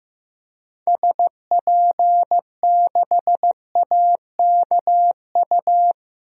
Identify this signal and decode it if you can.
Morse code, heard as a keyed tone, about 16 wpm, 700 Hz. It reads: SP6AKU
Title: DXCC-100 Call Signs - 6 Characters - 15wpm